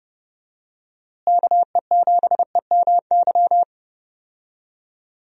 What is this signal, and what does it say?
Morse code, heard as a keyed tone, about 31 wpm, 700 Hz. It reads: KE7EMY
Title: USA Call Signs - 6 Characters - Repeated 3x - 30wpm